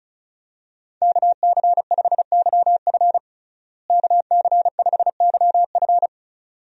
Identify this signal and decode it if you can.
Morse code, heard as a keyed tone, about 34 wpm, 700 Hz. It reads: KC5YF KC5YF
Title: USA Call Signs - 5 Characters - Repeated 2x - 35wpm